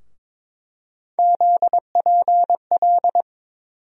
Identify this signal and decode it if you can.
Morse code, heard as a keyed tone, about 23 wpm, 700 Hz. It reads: ZPL
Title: USA Call Sign Postfixes - 3 Letters 22wpm